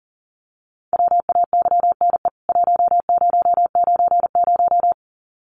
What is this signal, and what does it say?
Morse code, heard as a keyed tone, about 40 wpm, 700 Hz. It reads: WAYNE 1090
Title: CWT Exchange 40wpm